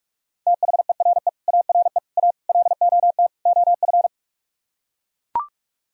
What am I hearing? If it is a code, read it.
Morse code, heard as a keyed tone, about 44 wpm, 700 Hz. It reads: THERE ARE A LOT OF